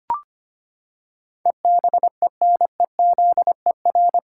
Morse code, heard as a keyed tone, about 25 wpm, 700 Hz. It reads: EBENEZER
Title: Pangram Challenge v2 - 25wpm